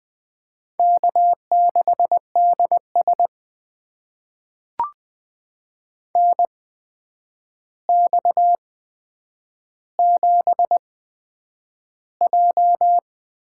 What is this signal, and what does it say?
Morse code, heard as a keyed tone, about 20 wpm, spaced as for 9 wpm, 700 Hz. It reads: K6DS NX7J
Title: USA Call Signs - 4 Characters - 8x Character Spacing - 20wpm